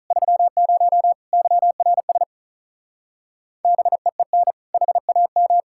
Morse code, heard as a keyed tone, about 44 wpm, 700 Hz. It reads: 30 YRS BEEN HAM